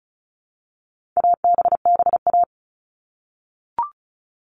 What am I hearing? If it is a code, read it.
Morse code, heard as a keyed tone, about 34 wpm, 700 Hz. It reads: ABBA
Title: All Cut Numbers - 4-Digit 35wpm